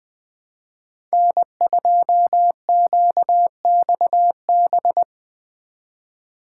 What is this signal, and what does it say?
Morse code, heard as a keyed tone, about 20 wpm, 700 Hz. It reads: N2QXB
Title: Speed Racing - USA 1x3 Call Signs 17wpm